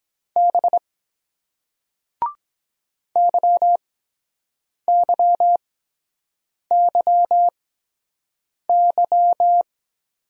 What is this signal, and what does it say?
Morse code, heard as a keyed tone, about 23 wpm, spaced as for 7 wpm, 700 Hz. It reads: B YYYY